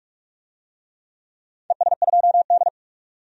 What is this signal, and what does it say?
Morse code, heard as a keyed tone, about 44 wpm, 700 Hz. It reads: ES2D
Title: DXCC-100 Call Signs - 4 Characters - 45wpm